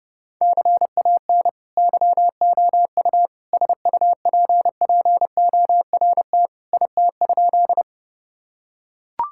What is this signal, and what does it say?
Morse code, heard as a keyed tone, about 32 wpm, 700 Hz. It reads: CAN YOU SUPPORT IT?
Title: Sentences from Top 1000 Words - Review Part 5 - 30wpm